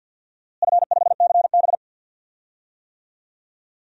Morse code, heard as a keyed tone, about 50 wpm, 700 Hz. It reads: F5CB
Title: DXCC-100 Call Signs - 4 Characters - Repeated 2x - 50wpm